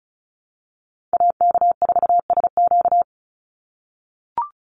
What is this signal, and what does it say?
Morse code, heard as a keyed tone, about 34 wpm, 700 Hz. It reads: AK4SQ
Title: Call Signs - 35wpm